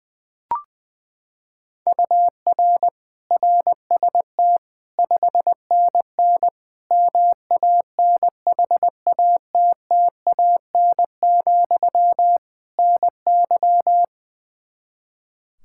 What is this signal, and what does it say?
Morse code, heard as a keyed tone, about 20 wpm, 700 Hz. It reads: UR RST 5NN MANHATTAN, NY
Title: Practice QSO - Exchange 1 - Beginning - 20wpm